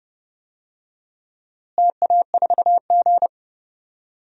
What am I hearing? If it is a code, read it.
Morse code, heard as a keyed tone, about 30 wpm, 700 Hz. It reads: TA4G
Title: DXCC-100 Call Signs - 4 Characters - Repeated 3x - 30wpm